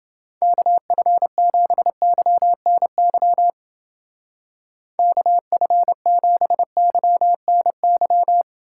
Morse code, heard as a keyed tone, about 28 wpm, 700 Hz. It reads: KF7YNY KF7YNY